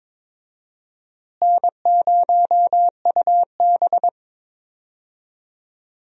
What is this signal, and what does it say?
Morse code, heard as a keyed tone, about 22 wpm, 700 Hz. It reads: N0UB
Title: USA Call Signs - 4 Characters - Repeated 2x - 22wpm